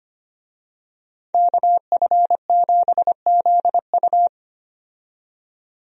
Morse code, heard as a keyed tone, about 25 wpm, 700 Hz. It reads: KF7ZU
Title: USA Call Signs - 5 Characters - 25wpm